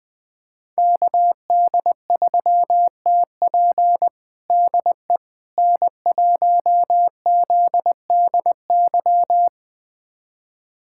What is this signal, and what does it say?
Morse code, heard as a keyed tone, about 21 wpm, 700 Hz. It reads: KD3TP DE N1ZDY
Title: USA Callsign DE Callsign - 5 Characters - 20wpm